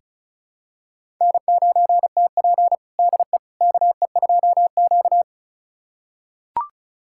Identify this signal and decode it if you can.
Morse code, heard as a keyed tone, about 34 wpm, 700 Hz. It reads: N9TP DE KE2Q
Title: USA Callsign DE Callsign - 4 Characters - 35wpm